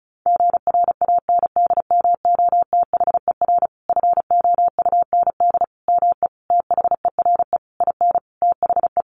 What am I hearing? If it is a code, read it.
Morse code, heard as a keyed tone, about 34 wpm, 700 Hz. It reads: GRANDMOTHER FOUND ME THERE IN THE
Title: Sentences from Top 1200 Words 35wpm